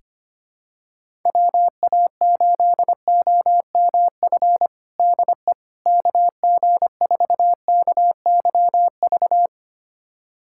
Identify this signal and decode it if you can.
Morse code, heard as a keyed tone, about 25 wpm, 700 Hz. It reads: WA8OMF DE KG4KYV
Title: USA Callsign DE Callsign - 6 Characters - 25wpm